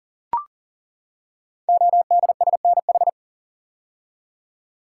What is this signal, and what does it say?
Morse code, heard as a keyed tone, about 39 wpm, 700 Hz. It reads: ODSNH
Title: Code Groups - 5 Letter:Numbers - 40wpm